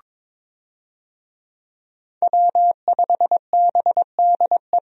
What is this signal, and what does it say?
Morse code, heard as a keyed tone, about 23 wpm, 700 Hz. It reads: W5BDE
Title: USA Call Signs - 5 Characters - Repeated 2x - 22wpm